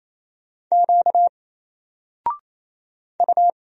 Morse code, heard as a keyed tone, about 29 wpm, 700 Hz. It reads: Q U
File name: Single-Letter-Number-28wpm